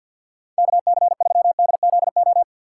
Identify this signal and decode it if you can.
Morse code, heard as a keyed tone, about 53 wpm, 700 Hz. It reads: KC3DZO
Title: USA Call Signs - 6 Characters - 6x Character Spacing - 50wpm